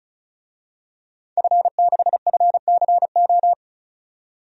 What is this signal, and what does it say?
Morse code, heard as a keyed tone, about 34 wpm, 700 Hz. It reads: F6FCO
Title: DXCC-100 Call Signs - 5 Characters - 35wpm